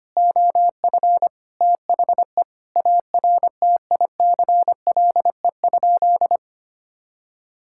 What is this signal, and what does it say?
Morse code, heard as a keyed tone, about 25 wpm, 700 Hz. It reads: OF THE ARTICLE?
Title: Sentences from Top 1000 Words - Review Part 5 - 25wpm